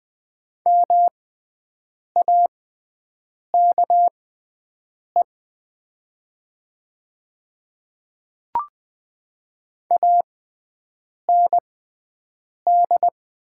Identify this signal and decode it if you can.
Morse code, heard as a keyed tone, about 20 wpm, spaced as for 7 wpm, 700 Hz. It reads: MAKE AND
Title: 6x Character Spacing - Top 100 Words - 20wpm